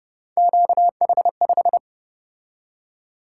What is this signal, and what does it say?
Morse code, heard as a keyed tone, about 32 wpm, 700 Hz. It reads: QH5